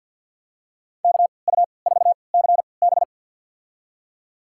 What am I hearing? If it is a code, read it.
Morse code, heard as a keyed tone, about 51 wpm, spaced as for 28 wpm, 700 Hz. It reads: KU4CB
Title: USA Call Signs - 3x Character Spacing - 50wpm